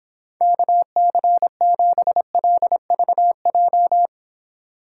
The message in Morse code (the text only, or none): KC7LVJ